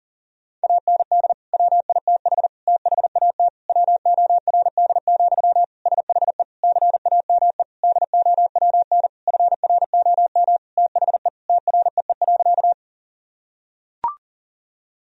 Morse code, heard as a keyed tone, about 40 wpm, 700 Hz. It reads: AND WITH THAT WORD, SHE CAME DOWN FROM THE TREE.
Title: Sentences from Top 1000 Words - Review Part 1 - 40wpm